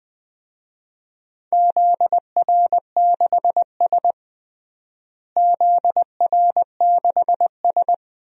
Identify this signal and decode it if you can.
Morse code, heard as a keyed tone, about 20 wpm, 700 Hz. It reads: ZR6S ZR6S
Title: DXCC-100 Call Signs - 4 Characters - Repeated 2x - 20wpm